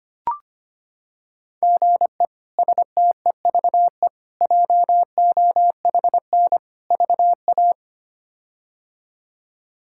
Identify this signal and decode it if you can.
Morse code, heard as a keyed tone, about 25 wpm, 700 Hz. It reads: GE STEVE JOHN VA
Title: K1USN SST QSO Practice - 25wpm